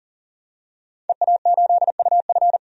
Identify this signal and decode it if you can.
Morse code, heard as a keyed tone, about 40 wpm, 700 Hz. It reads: EA8UF
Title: DXCC-100 Call Signs - 5 Characters - 40wpm